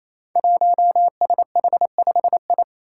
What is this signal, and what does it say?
Morse code, heard as a keyed tone, about 27 wpm, 700 Hz. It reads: 1SH5I
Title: Five Challenging Letter-Number Combinations 28wpm